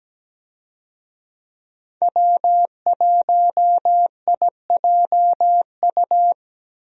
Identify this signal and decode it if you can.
Morse code, heard as a keyed tone, about 17 wpm, 700 Hz. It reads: W1IJU
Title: USA Call Signs - 5 Characters - Repeated 3x - 17wpm